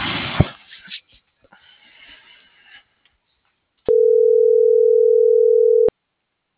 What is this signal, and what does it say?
Ringback tone (North America), 440 + 480 Hz.